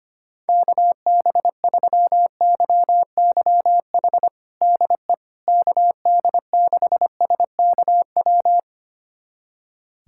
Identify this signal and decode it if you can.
Morse code, heard as a keyed tone, about 25 wpm, 700 Hz. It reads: KB3YYH DE KD6SKW
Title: USA Callsign DE Callsign - 6 Characters - 25wpm